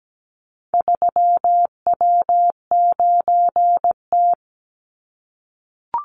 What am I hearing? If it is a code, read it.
Morse code, heard as a keyed tone, about 17 wpm, 700 Hz. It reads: 3W9T
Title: Call Signs - 17wpm